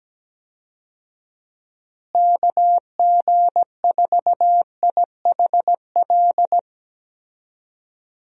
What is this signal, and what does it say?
Morse code, heard as a keyed tone, about 17 wpm, 700 Hz. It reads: KG4IHL
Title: USA Call Signs - 6 Characters - Repeated 2x - 17wpm